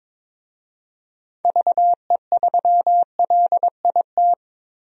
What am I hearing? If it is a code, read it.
Morse code, heard as a keyed tone, about 22 wpm, 700 Hz. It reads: VE3LIT
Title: DXCC-100 Call Signs - 6 Characters - Repeated 3x - 22wpm